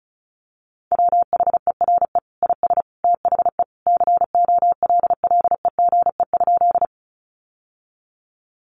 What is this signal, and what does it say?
Morse code, heard as a keyed tone, about 34 wpm, 700 Hz. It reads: WHERE IS THE COLLEGE?
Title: Sentences from Top 1300 Words 35wpm